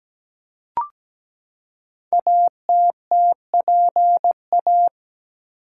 Morse code, heard as a keyed tone, about 17 wpm, 700 Hz. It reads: ATTPA
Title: Code Groups - 5 Letters - 17wpm